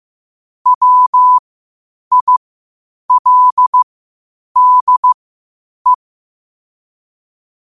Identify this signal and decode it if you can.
Morse code, heard as a keyed tone, about 15 wpm, spaced as for 8 wpm, 1000 Hz. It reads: WILDE
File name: qotdfiles / 8wpm